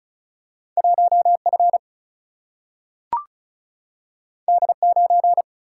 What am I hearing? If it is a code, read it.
Morse code, heard as a keyed tone, about 36 wpm, 700 Hz. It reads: 1F D9
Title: Single-Letter Single-Number 35wpm